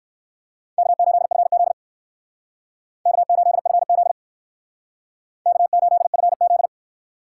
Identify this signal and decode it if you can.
Morse code, heard as a keyed tone, about 59 wpm, 700 Hz. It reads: K8FZ K8FZ K8FZ